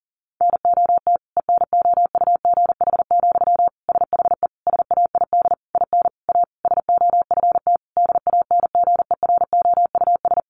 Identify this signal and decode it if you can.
Morse code, heard as a keyed tone, about 40 wpm, 700 Hz. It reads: NOT ENOUGH, SHE SAID IN A SOFT DANGEROUS
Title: Sentences from Top 1500 Words 40wpm